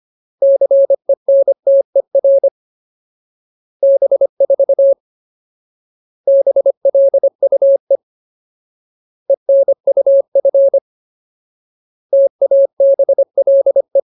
Morse code, heard as a keyed tone, about 25 wpm, 550 Hz. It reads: CENTER B4 BLUE ENUF TABLE
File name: Ultimate Word List / 25WPM Ultimate